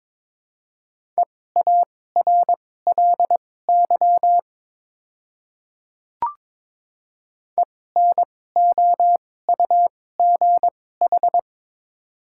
Morse code, heard as a keyed tone, about 23 wpm, spaced as for 16 wpm, 690 Hz. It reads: EARLY ENOUGH
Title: 2x Character Spacing - Top 500 Words - 22wpm